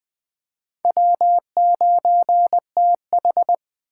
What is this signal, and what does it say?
Morse code, heard as a keyed tone, about 22 wpm, 700 Hz. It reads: W9TH